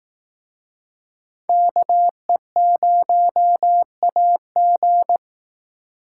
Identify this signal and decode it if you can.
Morse code, heard as a keyed tone, about 19 wpm, 700 Hz. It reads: KE0AG
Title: Speed Racing - USA 2x2 Call Signs 15wpm